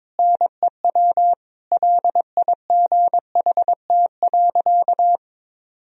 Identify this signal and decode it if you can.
Morse code, heard as a keyed tone, about 22 wpm, 700 Hz. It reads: NEW LIGHT.
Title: Sentences from Top 1000 Words - Review Part 1 - 22wpm